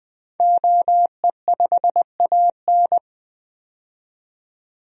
Morse code, heard as a keyed tone, about 20 wpm, 700 Hz. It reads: OE5AN
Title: DXCC-100 Call Signs - 5 Characters - 20wpm